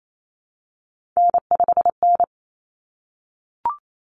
Morse code, heard as a keyed tone, about 28 wpm, 700 Hz. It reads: N5N
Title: Common Cut Numbers - 3-Digit 28wpm